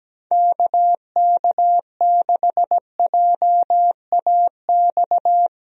Morse code, heard as a keyed tone, about 17 wpm, 700 Hz. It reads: KK6JAX
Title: USA Call Signs - 6 Characters - 17wpm